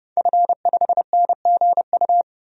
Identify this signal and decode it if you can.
Morse code, heard as a keyed tone, about 30 wpm, 700 Hz. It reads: F5NGU
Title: DXCC-100 Call Signs - 5 Characters - 30wpm